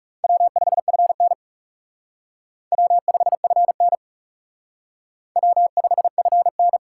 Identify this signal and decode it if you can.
Morse code, heard as a keyed tone, about 37 wpm, 700 Hz. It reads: W5FN W5FN W5FN
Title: Speed Racing - USA 1x2 Call Signs 30wpm